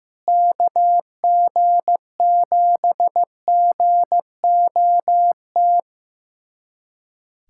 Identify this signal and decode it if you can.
Morse code, heard as a keyed tone, about 15 wpm, 700 Hz. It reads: KG7GOT
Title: USA Callsign DE Callsign 15wpm